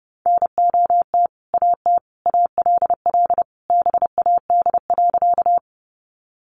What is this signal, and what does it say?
Morse code, heard as a keyed tone, about 30 wpm, 700 Hz. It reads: NOT AT ALL BAD.
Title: Sentences from Top 700 Words 30wpm